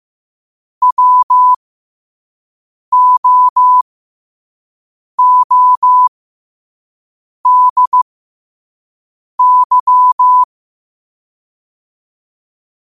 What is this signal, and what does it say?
Morse code, heard as a keyed tone, about 15 wpm, spaced as for 5 wpm, 1000 Hz. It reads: WOODY
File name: qotdfiles / 5wpm